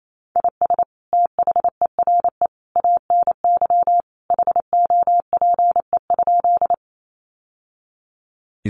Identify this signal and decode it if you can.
Morse code, heard as a keyed tone, about 28 wpm, 700 Hz. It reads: IS THERE ANY HOPE?
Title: Sentences from Top 1000 Words 28wpm